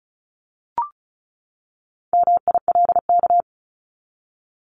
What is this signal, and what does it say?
Morse code, heard as a keyed tone, about 34 wpm, 700 Hz. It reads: MILK
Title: Top 1251-1500 Words 35wpm